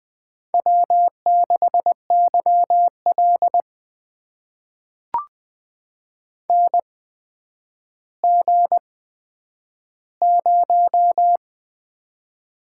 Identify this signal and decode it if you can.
Morse code, heard as a keyed tone, about 20 wpm, spaced as for 10 wpm, 700 Hz. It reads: W6YL NG0
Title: USA Call Signs - 4 Characters - 8x Character Spacing - 20wpm